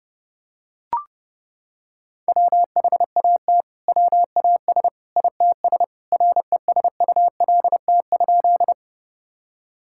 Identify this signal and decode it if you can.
Morse code, heard as a keyed tone, about 32 wpm, 700 Hz. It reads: WHAT WAS ITS RESULT?
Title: Sentences from Top 1000 Words - Review Part 5 - 30wpm